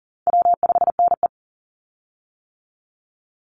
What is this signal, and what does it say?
Morse code, heard as a keyed tone, about 40 wpm, 700 Hz. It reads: W5NE
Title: Easy Call Signs - 40wpm